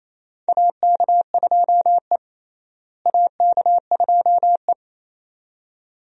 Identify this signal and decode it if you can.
Morse code, heard as a keyed tone, about 28 wpm, 700 Hz. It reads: AK2E AK2E
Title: USA Call Signs - 4 Characters - Repeated 2x - 28wpm